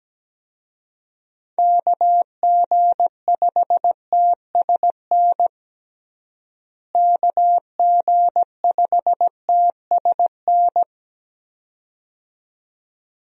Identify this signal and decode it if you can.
Morse code, heard as a keyed tone, about 17 wpm, 700 Hz. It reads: KG5TSN KG5TSN
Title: USA Call Signs - 6 Characters - Repeated 2x - 17wpm